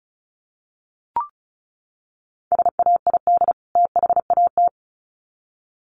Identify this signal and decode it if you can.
Morse code, heard as a keyed tone, about 34 wpm, 700 Hz. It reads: SAID THAT
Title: Sets of 2 Words 35wpm